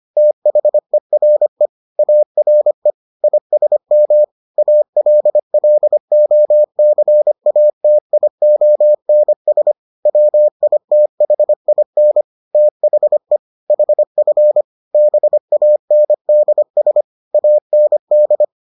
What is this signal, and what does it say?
Morse code, heard as a keyed tone, about 25 wpm, 600 Hz. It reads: THERE ARE ISM ALLOCATIONS WITHIN THE HF BANDS AND